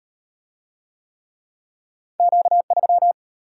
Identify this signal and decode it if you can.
Morse code, heard as a keyed tone, about 34 wpm, 700 Hz. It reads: Q3